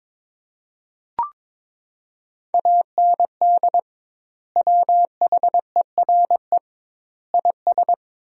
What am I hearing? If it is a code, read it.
Morse code, heard as a keyed tone, about 22 wpm, 700 Hz. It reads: AND WHERE IS
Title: 2x Word-Spacing - Sentences from Top 200 Words - 22wpm